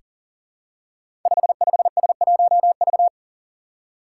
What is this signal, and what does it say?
Morse code, heard as a keyed tone, about 43 wpm, 700 Hz. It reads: 55S1V
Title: Five Challenging Letter-Number Combinations 40wpm